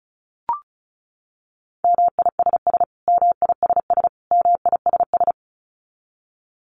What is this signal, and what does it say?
Morse code, heard as a keyed tone, about 34 wpm, 700 Hz. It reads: MISS MISS MISS
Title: Top 500 Words Repeated 35wpm